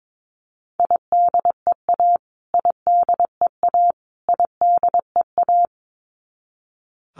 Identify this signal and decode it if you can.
Morse code, heard as a keyed tone, about 22 wpm, 700 Hz. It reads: IDEA IDEA IDEA
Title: Top 300 Words Repeated 22wpm